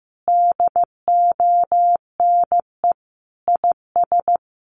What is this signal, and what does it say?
Morse code, heard as a keyed tone, about 15 wpm, 700 Hz. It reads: DONE IS
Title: Sentences from Top 500 Words 15wpm